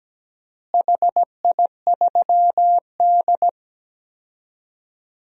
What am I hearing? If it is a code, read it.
Morse code, heard as a keyed tone, about 17 wpm, 700 Hz. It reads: HI3D
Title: DXCC-100 Call Signs - 4 Characters - Repeated 2x - 17wpm